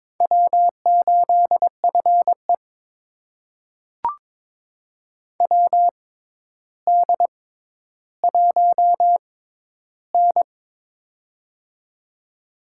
Morse code, heard as a keyed tone, about 21 wpm, spaced as for 11 wpm, 700 Hz. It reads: W8FE WD1N
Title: USA Call Signs - 4 Characters - 6x Character Spacing - 22wpm